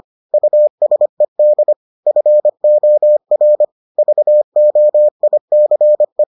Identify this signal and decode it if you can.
Morse code, heard as a keyed tone, about 25 wpm, 600 Hz. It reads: USED FOR VOICE